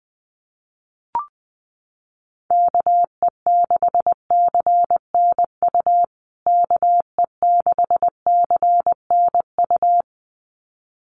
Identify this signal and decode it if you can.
Morse code, heard as a keyed tone, about 20 wpm, 700 Hz. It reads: KE6CNU KE6CNU
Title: USA Call Signs Repeated 2x 20wpm